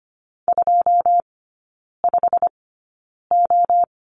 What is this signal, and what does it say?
Morse code, heard as a keyed tone, about 25 wpm, spaced as for 9 wpm, 700 Hz. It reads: 25O
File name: course / 25:10wpm / 093 Lesson 16e - Letter c - Three Characters 25wpm